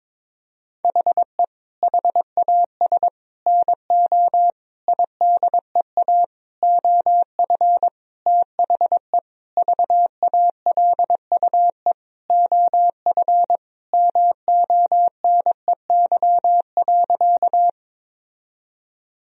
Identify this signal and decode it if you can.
Morse code, heard as a keyed tone, about 21 wpm, 700 Hz. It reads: HE HAS NO IDEA OF THE VALUE OF MONEY.
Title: Sentences from Top 1000 Words - Review Part 3 - 22wpm